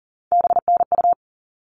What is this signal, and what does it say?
Morse code, heard as a keyed tone, about 40 wpm, 700 Hz. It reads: BNU
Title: All Cut Numbers - 3-Digit 40wpm